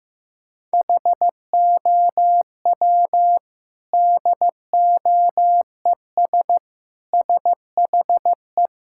Morse code, heard as a keyed tone, about 15 wpm, 700 Hz. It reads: HOW DOES SHE
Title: Sentences from Top 1000 Words - Review Part 6 - 15wpm